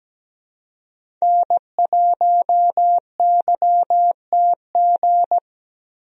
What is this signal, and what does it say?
Morse code, heard as a keyed tone, about 17 wpm, 700 Hz. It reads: N1YTG
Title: USA Call Signs - 5 Characters - 17wpm